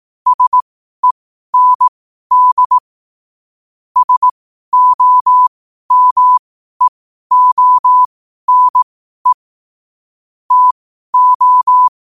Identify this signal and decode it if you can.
Morse code, heard as a keyed tone, about 18 wpm, spaced as for 12 wpm, 1000 Hz. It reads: SEND SOMEONE TO
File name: qotdfiles / 12wpm